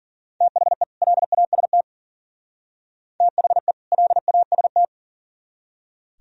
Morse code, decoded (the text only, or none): THE LAST THE LAST